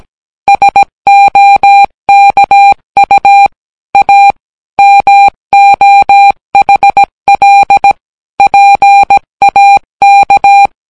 Morse code, heard as a keyed tone, about 17 wpm, 800 Hz. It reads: SOKU A MOHL PAK